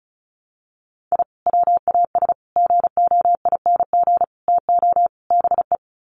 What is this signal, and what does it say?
Morse code, heard as a keyed tone, about 34 wpm, 700 Hz. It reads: I WAS GOING TO BE
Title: Sets of 5 Words 35wpm